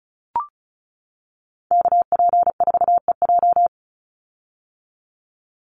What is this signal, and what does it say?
Morse code, heard as a keyed tone, about 34 wpm, 700 Hz. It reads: KP4EJ
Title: Call Signs - 35wpm